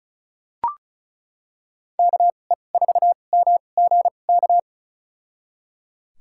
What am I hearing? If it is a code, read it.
Morse code, heard as a keyed tone, about 34 wpm, spaced as for 25 wpm, 700 Hz. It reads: KE4MGK
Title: USA Call Signs - 6 Characters - 2x Character Spacing - 35wpm